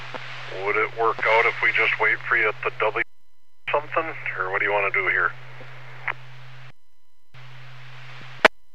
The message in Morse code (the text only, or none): none